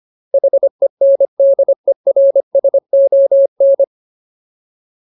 Morse code, heard as a keyed tone, about 25 wpm, 550 Hz. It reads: HENDERSON
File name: Ultimate Word List / 25WPM Ultimate